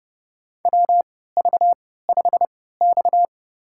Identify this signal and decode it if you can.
Morse code, heard as a keyed tone, about 31 wpm, spaced as for 17 wpm, 700 Hz. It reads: WV5X
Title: USA Call Signs - 4 Characters - 3x Character Spacing - 30wpm